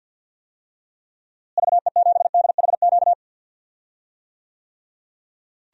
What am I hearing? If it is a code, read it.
Morse code, heard as a keyed tone, about 52 wpm, 700 Hz. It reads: VE7DHQ
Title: DXCC-100 Call Signs - 6 Characters - Repeated 3x - 50wpm